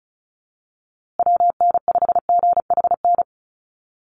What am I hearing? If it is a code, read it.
Morse code, heard as a keyed tone, about 34 wpm, 700 Hz. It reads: WN5GHN
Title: USA Call Signs Repeated 2x 35wpm